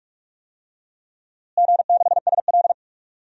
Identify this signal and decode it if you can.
Morse code, heard as a keyed tone, about 41 wpm, 700 Hz. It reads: G6SL
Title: DXCC-100 Call Signs - 4 Characters - Repeated 2x - 45wpm